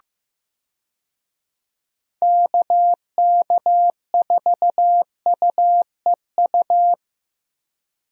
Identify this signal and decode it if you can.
Morse code, heard as a keyed tone, about 16 wpm, 700 Hz. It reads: KK4UEU
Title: USA Call Signs - 6 Characters - Repeated 2x - 15wpm